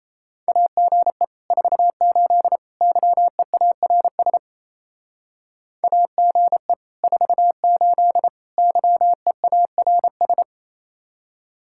Morse code, heard as a keyed tone, about 36 wpm, 700 Hz. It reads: AGE 48 YEARS AGE 48 YEARS